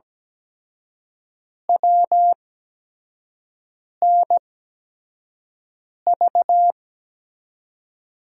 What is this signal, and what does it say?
Morse code, heard as a keyed tone, about 17 wpm, spaced as for 5 wpm, 700 Hz. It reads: WNV
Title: USA Call Signs - 6 Characters - 8x Character Spacing - 17wpm